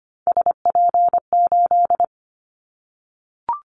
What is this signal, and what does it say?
Morse code, heard as a keyed tone, about 25 wpm, 700 Hz. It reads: SP8
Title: Callsign Prefixes for Top 25 DXCC - 25wpm